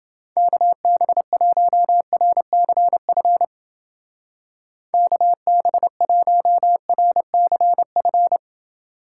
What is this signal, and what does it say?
Morse code, heard as a keyed tone, about 30 wpm, 700 Hz. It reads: KB1RCF KB1RCF